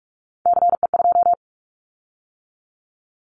Morse code, heard as a keyed tone, about 44 wpm, 700 Hz. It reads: CE2
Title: Callsign Prefixes for Top 100 DXCC - 45wpm